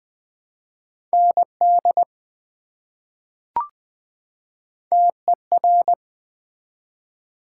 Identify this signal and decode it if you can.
Morse code, heard as a keyed tone, about 20 wpm, 700 Hz. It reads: ND TER
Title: ARRL Sections v2 - 20wpm